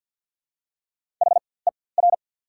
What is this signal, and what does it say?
Morse code, heard as a keyed tone, about 48 wpm, spaced as for 23 wpm, 690 Hz. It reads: HER
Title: 4x Character Spacing - Top 100 Words - 50wpm